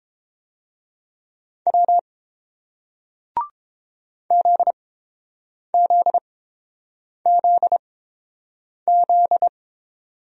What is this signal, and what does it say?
Morse code, heard as a keyed tone, about 28 wpm, spaced as for 7 wpm, 700 Hz. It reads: W ZZZZ